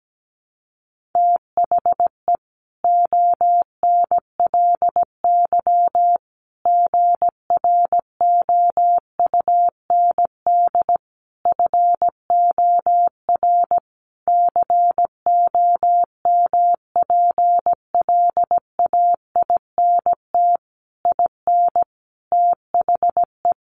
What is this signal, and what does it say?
Morse code, heard as a keyed tone, about 17 wpm, 700 Hz. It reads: THE ONLY GROUND FOR COMPLAINT IN THE
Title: Sentences from Top 1400 Words 17wpm